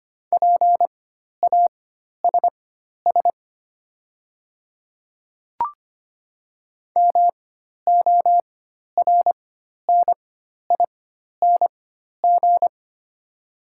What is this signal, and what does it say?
Morse code, heard as a keyed tone, about 25 wpm, spaced as for 12 wpm, 690 Hz. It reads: PASS MORNING
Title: 4x Character Spacing - Top 500 Words - 25wpm